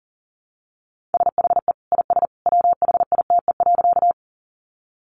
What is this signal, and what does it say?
Morse code, heard as a keyed tone, about 40 wpm, 700 Hz. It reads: SHE IS WHITE.
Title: Easy Phrases 40wpm